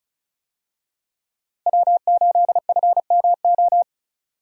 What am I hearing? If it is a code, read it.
Morse code, heard as a keyed tone, about 36 wpm, 700 Hz. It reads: W8FMO
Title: USA Call Signs - 5 Characters - Repeated 2x - 35wpm